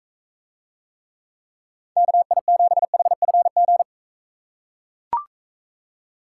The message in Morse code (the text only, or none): KI7HFG